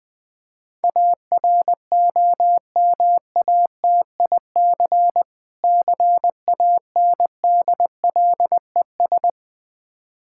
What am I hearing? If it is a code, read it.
Morse code, heard as a keyed tone, about 20 wpm, 700 Hz. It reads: AROMATIC CANDLES
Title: Difficult Words with Hints - 20wpm